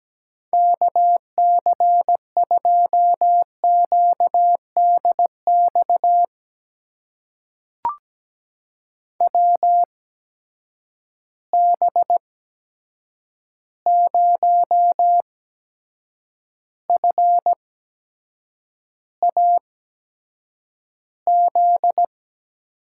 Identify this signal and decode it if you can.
Morse code, heard as a keyed tone, about 17 wpm, spaced as for 7 wpm, 700 Hz. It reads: KC2QDX WB0FAZ
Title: USA Call Signs - 6 Characters - 8x Character Spacing - 17wpm